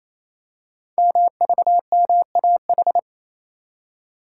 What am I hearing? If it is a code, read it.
Morse code, heard as a keyed tone, about 28 wpm, 700 Hz. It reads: MVMAH
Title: Code Groups - 5 Letters - 28wpm